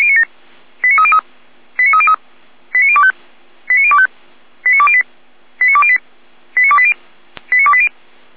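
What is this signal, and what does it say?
Five-tone selective call (ZVEI), heard as quick runs of five tones, tones of about 70 ms. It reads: 89303 89303 89025 89025 89208 89208 89280 89280
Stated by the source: ZVEI five-tone selective call received on SDR.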